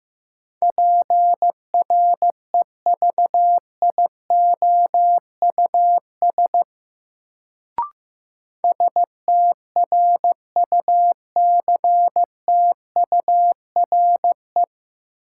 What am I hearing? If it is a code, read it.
Morse code, heard as a keyed tone, about 15 wpm, 700 Hz. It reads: PREVIOUS STRUCTURE
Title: Sentences from Top 1500 Words - New Word Drill Alt - 15wpm